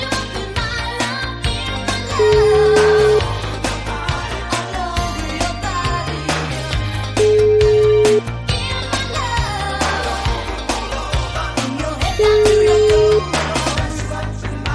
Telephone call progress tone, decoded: ringback tone

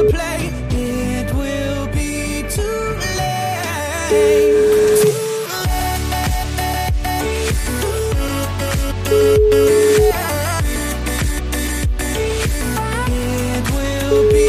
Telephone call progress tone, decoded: ringback tone

